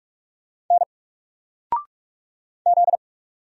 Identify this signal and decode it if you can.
Morse code, heard as a keyed tone, about 44 wpm, 700 Hz. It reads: N Z
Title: Single Letter-Number Alt 45wpm